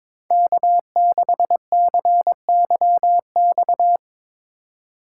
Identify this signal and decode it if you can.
Morse code, heard as a keyed tone, about 24 wpm, 700 Hz. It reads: K6CYX
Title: Speed Racing - USA 1x3 Call Signs 22wpm